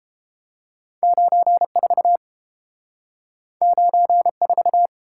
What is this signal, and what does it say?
Morse code, heard as a keyed tone, about 30 wpm, 700 Hz. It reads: 94 94